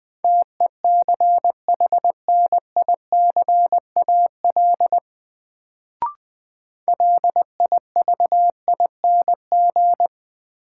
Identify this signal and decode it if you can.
Morse code, heard as a keyed tone, about 20 wpm, 700 Hz. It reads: TECHNICAL LIVING
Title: Sentences from Top 1100 Words - New Word Drill Alt - 20wpm